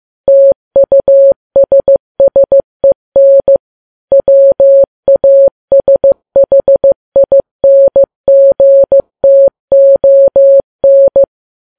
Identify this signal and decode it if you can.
Morse code, heard as a keyed tone, about 15 wpm, 550 Hz. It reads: TUSSEN WASHINGTON
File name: Files Web tls / Morse on9fv / morse15kps